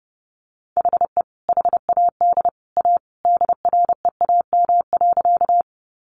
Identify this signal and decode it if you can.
Morse code, heard as a keyed tone, about 30 wpm, 700 Hz. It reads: HE HAD A DREAM.
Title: Sentences from Top 1500 Words 30wpm